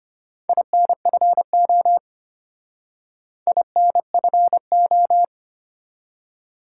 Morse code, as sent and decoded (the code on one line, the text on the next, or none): .. -. ..-. --- / .. -. ..-. ---
INFO INFO